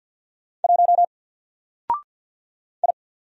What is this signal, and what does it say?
Morse code, heard as a keyed tone, about 48 wpm, 700 Hz. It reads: 1 I
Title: Single Character (Letter:Number:Punctuation:Prosign) Alt 50wpm